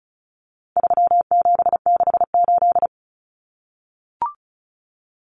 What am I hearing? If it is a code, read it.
Morse code, heard as a keyed tone, about 34 wpm, 700 Hz. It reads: 3768
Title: Numbers - 4-Digit 35wpm